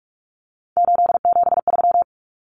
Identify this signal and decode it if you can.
Morse code, heard as a keyed tone, about 44 wpm, 700 Hz. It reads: 873
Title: Common Cut Numbers - 3-Digit 45wpm